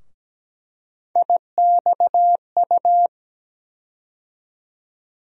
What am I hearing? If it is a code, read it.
Morse code, heard as a keyed tone, about 17 wpm, 700 Hz. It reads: IXU
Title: USA Call Sign Postfixes - 3 Letters 17wpm